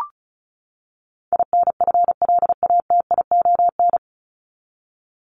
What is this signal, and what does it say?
Morse code, heard as a keyed tone, about 34 wpm, 700 Hz. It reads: INFLATION
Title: Top 1001-1250 Words 35wpm